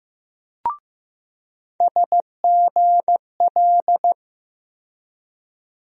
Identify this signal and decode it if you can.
Morse code, heard as a keyed tone, about 15 wpm, 700 Hz. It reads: SGL
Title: USA Call Sign Postfixes - 3 Letters 15wpm